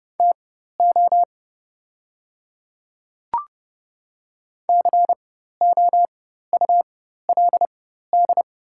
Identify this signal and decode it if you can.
Morse code, heard as a keyed tone, about 30 wpm, spaced as for 14 wpm, 700 Hz. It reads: TO COULD
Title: 4x Character Spacing - Top 100 Words - 30wpm